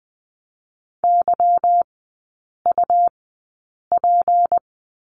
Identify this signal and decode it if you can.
Morse code, heard as a keyed tone, about 20 wpm, spaced as for 8 wpm, 700 Hz. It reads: YUP
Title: 152 Lesson 25e - Letter p - Three Characters 20wpm